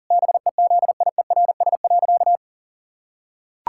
Morse code, heard as a keyed tone, about 40 wpm, 700 Hz. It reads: BEZIERS.
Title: Pangram Challenge v2 - 40wpm